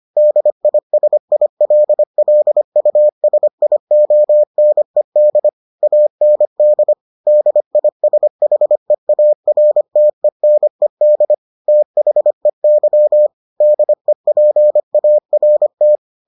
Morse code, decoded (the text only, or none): DISILLUSIONED AND DISHEARTENED THEY DEPART